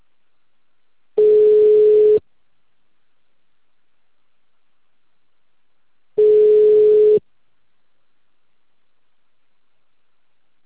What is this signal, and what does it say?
Ringback tone (Europe, 425 Hz).